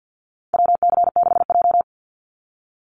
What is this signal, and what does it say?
Morse code, heard as a keyed tone, about 48 wpm, 700 Hz. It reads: WC6J
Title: NA SOTA Activator Callsigns 50wpm